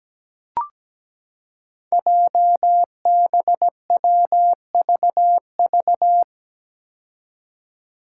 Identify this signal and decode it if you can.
Morse code, heard as a keyed tone, about 17 wpm, 700 Hz. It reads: JBWVV
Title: Five Challenging Letter-Number Combinations 17wpm